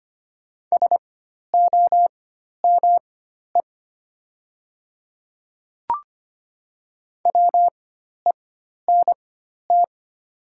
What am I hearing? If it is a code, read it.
Morse code, heard as a keyed tone, about 25 wpm, spaced as for 12 wpm, 700 Hz. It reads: SOME WENT
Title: 4x Character Spacing - Top 200 Words - 25wpm